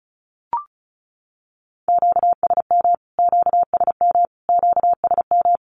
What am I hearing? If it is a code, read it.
Morse code, heard as a keyed tone, about 34 wpm, 700 Hz. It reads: QSM QSM QSM
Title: Q-Codes Repeated 3x 35wpm